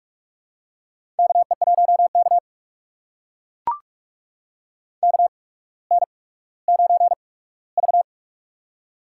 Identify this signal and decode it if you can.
Morse code, heard as a keyed tone, about 44 wpm, spaced as for 19 wpm, 700 Hz. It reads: KE1K KN9V
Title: USA Call Signs - 4 Characters - 8x Character Spacing - 45wpm